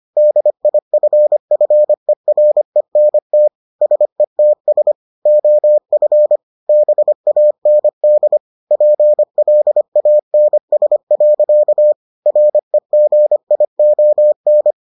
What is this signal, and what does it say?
Morse code, heard as a keyed tone, about 25 wpm, 600 Hz. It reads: DIFFERENT SETS OF BAND PLANS. REGION